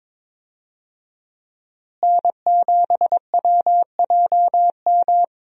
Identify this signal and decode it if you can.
Morse code, heard as a keyed tone, about 22 wpm, 700 Hz. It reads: N7WJM
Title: USA Call Signs - 5 Characters - Repeated 2x - 22wpm